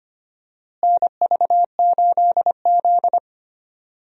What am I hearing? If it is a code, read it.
Morse code, heard as a keyed tone, about 25 wpm, 700 Hz. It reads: NV8Z
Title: USA Call Signs - 4 Characters - Repeated 3x - 25wpm